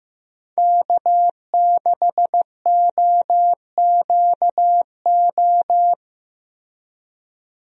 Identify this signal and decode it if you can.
Morse code, heard as a keyed tone, about 15 wpm, 700 Hz. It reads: K6OQO
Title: USA Call Signs - 5 Characters - Repeated 3x - 15wpm